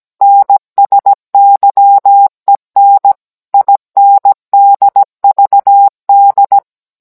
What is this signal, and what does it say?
Morse code, heard as a keyed tone, about 17 wpm, 800 Hz. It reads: NSYEN INDVD